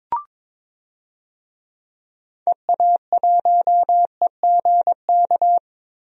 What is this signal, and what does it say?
Morse code, heard as a keyed tone, about 22 wpm, 700 Hz. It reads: EA1EGK
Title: DXCC-100 Call Signs - 6 Characters - Repeated 2x - 22wpm